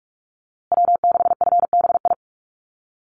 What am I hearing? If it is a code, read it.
Morse code, heard as a keyed tone, about 44 wpm, 690 Hz. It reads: W6FBI
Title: USA Call Signs Repeated 2x 45wpm